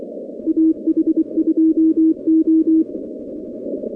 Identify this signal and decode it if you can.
Morse code, heard as a keyed tone, about 23 wpm, 320 Hz. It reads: AH2O
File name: contests / 2019-cw-wpx-kt5j-k5pi- -2330 0-46